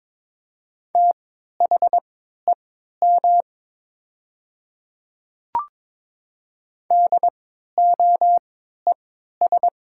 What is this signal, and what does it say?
Morse code, heard as a keyed tone, about 23 wpm, spaced as for 13 wpm, 700 Hz. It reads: THEM DOES
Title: 3x Character Spacing - Top 200 Words - 22wpm